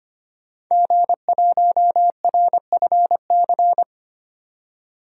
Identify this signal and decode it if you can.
Morse code, heard as a keyed tone, about 25 wpm, 700 Hz. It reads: G1RFC
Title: DXCC-100 Call Signs - 5 Characters - 25wpm